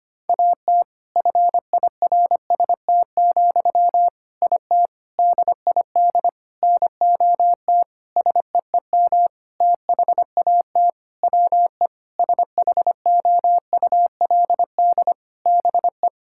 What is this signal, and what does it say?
Morse code, heard as a keyed tone, about 25 wpm, 700 Hz. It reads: AT FIRST, IT DID NOT SEEM THAT WE SHOULD BE
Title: Sentences from Top 1000 Words - Review Part 6 - 25wpm